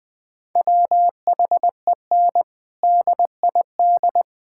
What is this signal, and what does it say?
Morse code, heard as a keyed tone, about 21 wpm, 700 Hz. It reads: WHEN DID
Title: Sentences from Top 1600 Words 20wpm